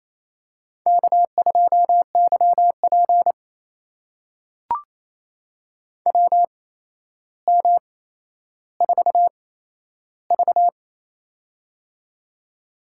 Morse code, heard as a keyed tone, about 28 wpm, spaced as for 12 wpm, 700 Hz. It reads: K2YP WM4V
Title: USA Call Signs - 4 Characters - 8x Character Spacing - 28wpm